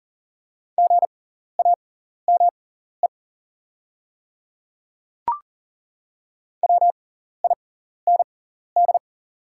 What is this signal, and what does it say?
Morse code, heard as a keyed tone, about 39 wpm, spaced as for 14 wpm, 690 Hz. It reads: GAME WIND
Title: 6x Character Spacing - Top 500 Words - 40wpm